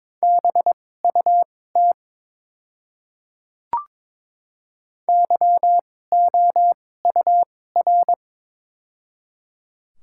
Morse code, heard as a keyed tone, about 22 wpm, spaced as for 16 wpm, 700 Hz. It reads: BUT YOUR
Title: 2x Character Spacing - Top 200 Words - 22wpm